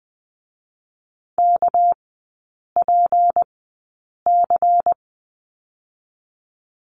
Morse code, heard as a keyed tone, about 20 wpm, spaced as for 8 wpm, 700 Hz. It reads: KPC
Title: 218 Lesson 35e - Number 0 - Three Characters 20wpm